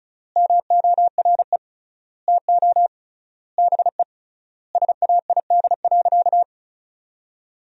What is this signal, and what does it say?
Morse code, heard as a keyed tone, about 34 wpm, 700 Hz. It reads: MORE TO BE SAID.
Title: 3x Word-Spacing - Sentences from Top 100 Words - 35wpm